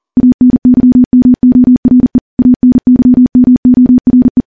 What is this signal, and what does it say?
Morse code, heard as a keyed tone, about 40 wpm, 250 Hz. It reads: ANYMORE ANYMORE
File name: morse / morse-training-40WPM